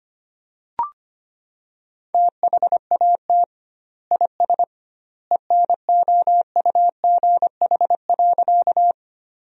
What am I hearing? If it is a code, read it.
Morse code, heard as a keyed tone, about 25 wpm, 700 Hz. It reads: THAT IS ENOUGH.
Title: 2x Word-Spacing - Sentences from Top 400 Words - 25wpm